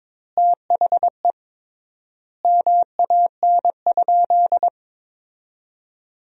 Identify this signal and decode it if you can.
Morse code, heard as a keyed tone, about 21 wpm, 700 Hz. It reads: THE MAN?
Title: 3x Word-Spacing - Sentences from Top 200 Words - 22wpm